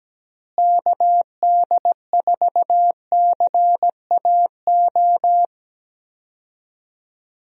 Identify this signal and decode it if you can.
Morse code, heard as a keyed tone, about 17 wpm, 700 Hz. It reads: KD4CAO
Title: USA Call Signs - 6 Characters - Repeated 2x - 17wpm